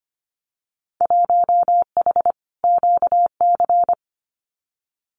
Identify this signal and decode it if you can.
Morse code, heard as a keyed tone, about 25 wpm, 700 Hz. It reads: 1H QC
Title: Winter Field Day Exchanges 25wpm